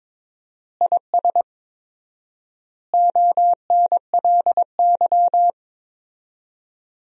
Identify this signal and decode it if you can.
Morse code, heard as a keyed tone, about 22 wpm, 700 Hz. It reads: IS ONLY
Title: 4x Word-Spacing - Sentences from Top 300 Words - 22wpm